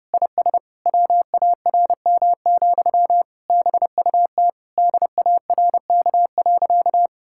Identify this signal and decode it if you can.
Morse code, heard as a keyed tone, about 30 wpm, 700 Hz. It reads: IS WARM, BUT DARK.
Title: Sentences from Top 1600 Words 30wpm